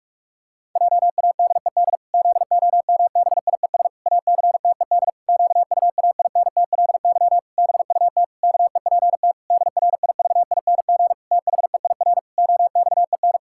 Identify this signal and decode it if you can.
Morse code, heard as a keyed tone, about 46 wpm, 700 Hz. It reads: JADED ZOMBIES ACTED QUAINTLY BUT KEPT DRIVING THEIR OXEN
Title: Pangram Challenge v2 - 45wpm